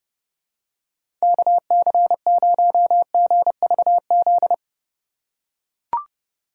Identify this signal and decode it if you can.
Morse code, heard as a keyed tone, about 32 wpm, 700 Hz. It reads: KC0GVZ
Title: USA Call Signs - 2x Character Spacing - 30wpm